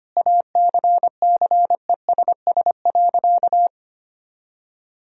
Morse code, heard as a keyed tone, about 25 wpm, 700 Hz. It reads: ACCESS.
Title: Sentences from Top 1000 Words - Review Part 4 - 25wpm